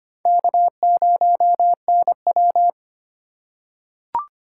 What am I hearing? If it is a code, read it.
Morse code, heard as a keyed tone, about 25 wpm, 700 Hz. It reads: K0NW
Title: K1USN SST QSO Practice - 25wpm